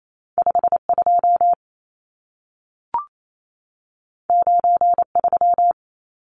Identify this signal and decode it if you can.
Morse code, heard as a keyed tone, about 28 wpm, 700 Hz. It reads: 52 93
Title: Numbers - 2-Digit 28wpm